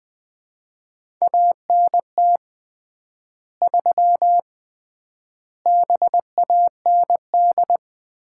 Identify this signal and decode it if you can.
Morse code, heard as a keyed tone, about 20 wpm, 700 Hz. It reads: ANT 3 BAND
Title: Common QSO Phrases - 3x Word-Spacing - 20wpm